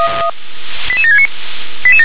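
Five-tone selective call (ZVEI), heard as quick runs of five tones, tones of about 70 ms. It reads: 88769
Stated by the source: Five-tone ZVEI selective call with code 88713, received by SDR.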